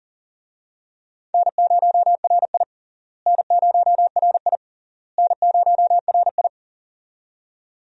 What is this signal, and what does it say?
Morse code, heard as a keyed tone, about 40 wpm, 700 Hz. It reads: N0RI N0RI N0RI
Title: USA Call Signs - 4 Characters - Repeated 3x - 40wpm